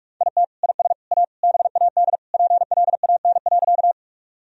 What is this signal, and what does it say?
Morse code, heard as a keyed tone, about 42 wpm, 700 Hz. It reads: IT IS A BAD PLAN.
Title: Sentences from Top 1000 Words - Review Part 2 - 45wpm